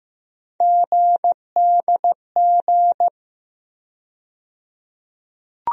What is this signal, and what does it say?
Morse code, heard as a keyed tone, about 15 wpm, 700 Hz. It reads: GDG
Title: USA Call Sign Postfixes - 3 Letters 15wpm